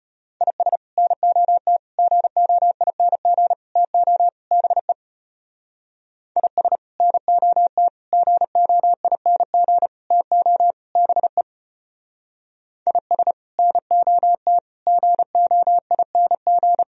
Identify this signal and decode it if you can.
Morse code, heard as a keyed tone, about 32 wpm, 700 Hz. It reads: IS NOT GOING TO BE IS NOT GOING TO BE IS NOT GOING